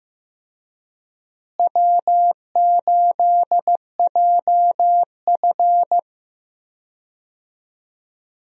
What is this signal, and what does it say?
Morse code, heard as a keyed tone, about 15 wpm, 700 Hz. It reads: W8JF
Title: USA Call Signs - 4 Characters - Repeated 2x - 15wpm